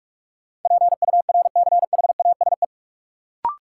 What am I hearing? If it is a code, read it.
Morse code, heard as a keyed tone, about 42 wpm, 700 Hz. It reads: PURCHASE
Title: Sentences from Top 800 Words - New Word Drill - 45wpm